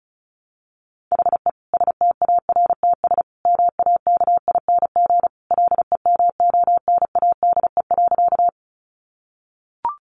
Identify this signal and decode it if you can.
Morse code, heard as a keyed tone, about 34 wpm, 700 Hz. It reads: HE STARTS MAKING LEMONADE.
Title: Additional Easy Phrases 35wpm